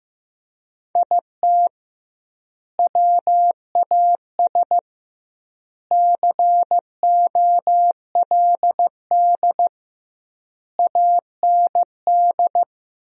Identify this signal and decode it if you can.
Morse code, heard as a keyed tone, about 15 wpm, 700 Hz. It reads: IT WAS COLD AND
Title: 2x Word-Spacing - Sentences from Top 500 Words - 15wpm